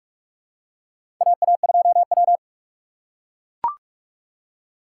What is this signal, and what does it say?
Morse code, heard as a keyed tone, about 42 wpm, 700 Hz. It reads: AA2W
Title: USA Call Signs - 3x Character Spacing - 45wpm